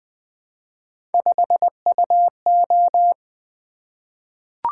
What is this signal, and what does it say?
Morse code, heard as a keyed tone, about 20 wpm, 700 Hz. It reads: 5UO
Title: Three Letter-Number Combinations - 20wpm